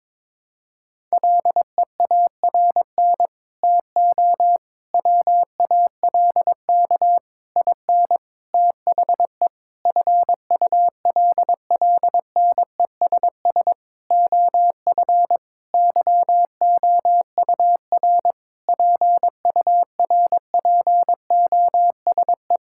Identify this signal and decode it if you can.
Morse code, heard as a keyed tone, about 22 wpm, 700 Hz. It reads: LEARN TO WALK IN THE FULLNESS OF YOUR PURPOSE
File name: Sentences with 9-Word Phrases - 22wpm